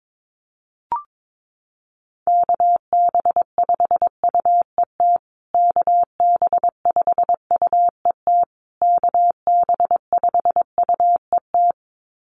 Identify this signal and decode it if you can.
Morse code, heard as a keyed tone, about 22 wpm, 700 Hz. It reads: KB5UET KB5UET KB5UET
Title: USA Call Signs Repeated 3x 22wpm